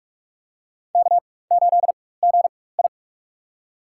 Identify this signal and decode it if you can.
Morse code, heard as a keyed tone, about 46 wpm, spaced as for 21 wpm, 700 Hz. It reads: K8GI
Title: USA Call Signs - 4 Characters - 4x Character Spacing - 45wpm